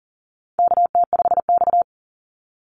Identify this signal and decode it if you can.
Morse code, heard as a keyed tone, about 39 wpm, 700 Hz. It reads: KT5X
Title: NA SOTA Chaser Callsigns 40wpm